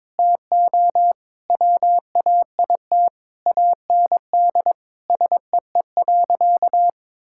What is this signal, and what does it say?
Morse code, heard as a keyed tone, about 21 wpm, 700 Hz. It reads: TO WAIT AND SEE.
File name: Sentences with 7-Word Phrases - 22wpm